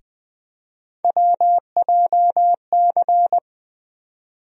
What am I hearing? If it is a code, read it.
Morse code, heard as a keyed tone, about 20 wpm, 700 Hz. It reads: WJC
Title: USA Call Sign Postfixes - 3 Letters 20wpm